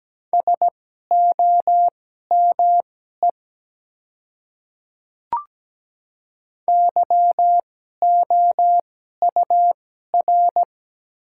Morse code, heard as a keyed tone, about 17 wpm, spaced as for 12 wpm, 700 Hz. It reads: SOME YOUR
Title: 2x Character Spacing - Top 100 Words - 17wpm